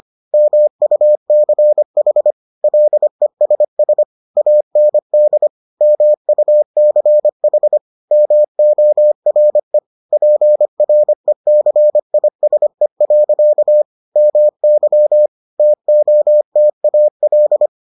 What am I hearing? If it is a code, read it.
Morse code, heard as a keyed tone, about 25 wpm, 600 Hz. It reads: MUCH LESS AND MUCH MORE PRECISE. MY TOTAL